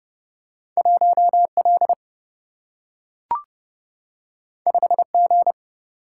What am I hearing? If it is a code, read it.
Morse code, heard as a keyed tone, about 30 wpm, 700 Hz. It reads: 1 L 5 G
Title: Single-Letter Single-Number 30wpm